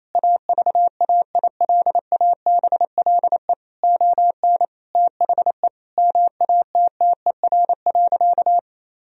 Morse code, heard as a keyed tone, about 28 wpm, 700 Hz. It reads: AVAILABLE ON THE MATTER.
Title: Sentences from Top 1000 Words - Review Part 3 - 28wpm